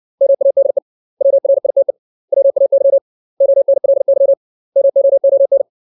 Morse code, heard as a keyed tone, about 60 wpm, 550 Hz. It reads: MADE WRITE WAY ONLY MOON
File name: 632 words / 632 Words 60WPM